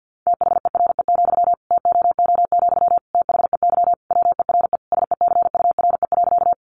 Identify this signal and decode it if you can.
Morse code, heard as a keyed tone, about 50 wpm, 700 Hz. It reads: THERE, TOO, THEY WERE SECURE.
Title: Sentences from Top 1200 Words 50wpm